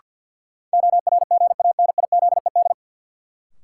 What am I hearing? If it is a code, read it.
Morse code, heard as a keyed tone, about 51 wpm, 700 Hz. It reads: ORGANIZED
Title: Sentences from Top 1300 Words - New Word Drill Alt - 50wpm